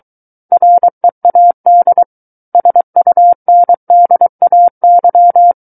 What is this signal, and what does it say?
Morse code, heard as a keyed tone, about 23 wpm, 700 Hz. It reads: READ SUNDAY
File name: Book One / 14-Chapter 14 - 23 WPM